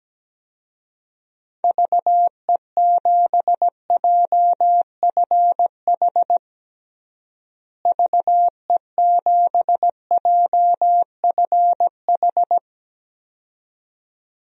Morse code, heard as a keyed tone, about 17 wpm, 700 Hz. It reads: VE7JFH VE7JFH
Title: DXCC-100 Call Signs - 6 Characters - Repeated 2x - 17wpm